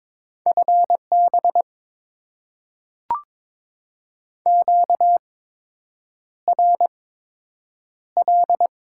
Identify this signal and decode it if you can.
Morse code, heard as a keyed tone, about 22 wpm, spaced as for 8 wpm, 700 Hz. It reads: FB QRL
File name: Top 20 QSO Elements - 8x Character Spacing - 22wpm